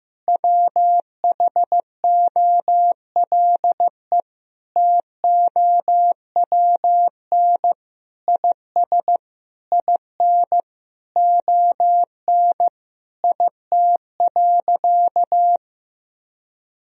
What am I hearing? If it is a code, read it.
Morse code, heard as a keyed tone, about 15 wpm, 700 Hz. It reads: WHOLE TOWN IS IN ON IT.
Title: Sentences from Top 1000 Words - Review Part 4 - 15wpm